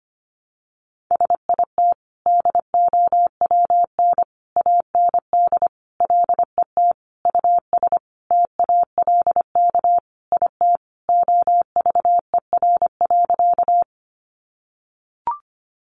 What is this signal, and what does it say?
Morse code, heard as a keyed tone, about 25 wpm, 700 Hz. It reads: SIT DOWN AND LET US TALK IT OVER.
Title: Sentences from Top 500 Words 25wpm